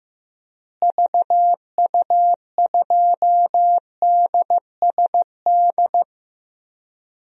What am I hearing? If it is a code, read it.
Morse code, heard as a keyed tone, about 15 wpm, 700 Hz. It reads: VU2DSD
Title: DXCC-100 Call Signs - 6 Characters - Repeated 2x - 15wpm